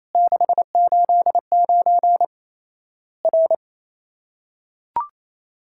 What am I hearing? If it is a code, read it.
Morse code, heard as a keyed tone, about 28 wpm, 700 Hz. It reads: 689 R
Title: Worked All Europe DX Contest - QTC Exchange - 28wpm